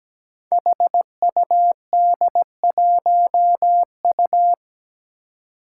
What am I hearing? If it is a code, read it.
Morse code, heard as a keyed tone, about 17 wpm, 700 Hz. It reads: HUD1U
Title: Five Challenging Letter-Number Combinations 17wpm